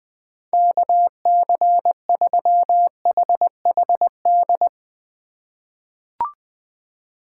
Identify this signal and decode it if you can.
Morse code, heard as a keyed tone, about 21 wpm, 700 Hz. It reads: KC3HHD
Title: USA Call Signs - 6 Characters - 2x Character Spacing - 20wpm